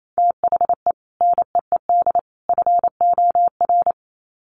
Morse code, heard as a keyed tone, about 28 wpm, 700 Hz. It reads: THE NEED FOR
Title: Sets of 3 Words 28wpm